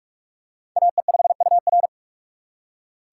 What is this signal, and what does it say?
Morse code, heard as a keyed tone, about 47 wpm, 700 Hz. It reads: AE5UR
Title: USA Call Signs - 5 Characters - 3x Character Spacing - 45wpm